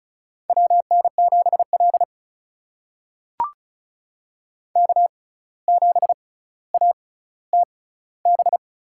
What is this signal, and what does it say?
Morse code, heard as a keyed tone, about 34 wpm, spaced as for 17 wpm, 700 Hz. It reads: WN7L K7ATB
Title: USA Call Signs - 6x Character Spacing - 35wpm